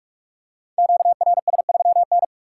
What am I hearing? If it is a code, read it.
Morse code, heard as a keyed tone, about 44 wpm, 700 Hz. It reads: QRS3N
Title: Code Groups - 5 Letter:Numbers - 45wpm